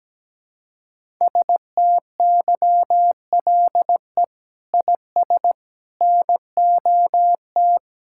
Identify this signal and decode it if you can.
Morse code, heard as a keyed tone, about 17 wpm, 700 Hz. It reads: STYLE IS NOT
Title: Sentences from Top 1000 Words - Review Part 5 - 17wpm